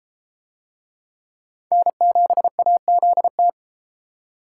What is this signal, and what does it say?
Morse code, heard as a keyed tone, about 32 wpm, 700 Hz. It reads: N7AZT